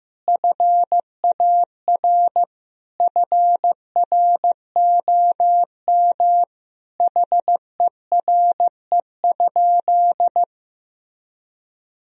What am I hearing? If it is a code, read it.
Morse code, heard as a keyed tone, about 16 wpm, 700 Hz. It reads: FAR FROM HERE?
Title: Sentences from Top 1000 Words - Review Part 4 - 15wpm